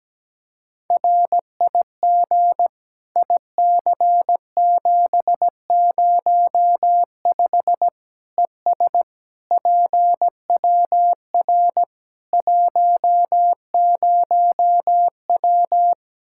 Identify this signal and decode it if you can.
Morse code, heard as a keyed tone, about 17 wpm, 700 Hz. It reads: RIG IC705 ES PWR 10W
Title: Practice QSO - Exchange 2 - Info - 17wpm